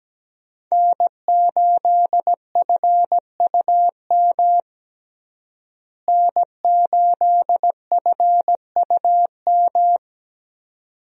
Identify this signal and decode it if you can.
Morse code, heard as a keyed tone, about 17 wpm, 700 Hz. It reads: N8FUM N8FUM
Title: USA Call Signs - 5 Characters - Repeated 2x - 17wpm